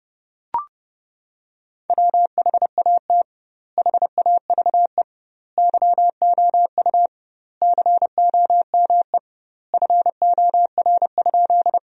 Morse code, heard as a keyed tone, about 32 wpm, 700 Hz. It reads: WHAT HAVE YOU COME FOR?
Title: 2x Word-Spacing - Sentences from Top 100 Words - 30wpm